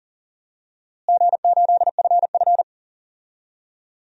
Morse code, heard as a keyed tone, about 38 wpm, 700 Hz. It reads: G8FF
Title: DXCC-100 Call Signs - 4 Characters - Repeated 2x - 40wpm